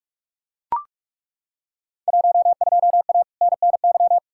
Morse code, heard as a keyed tone, about 42 wpm, 700 Hz. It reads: 12A NNY
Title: Field Day Exchanges Encore - 45wpm